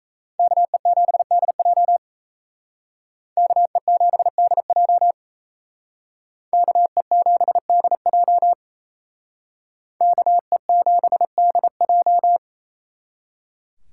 Morse code, heard as a keyed tone, about 36 wpm, 700 Hz. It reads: KE7DJ KE7DJ KE7DJ KE7DJ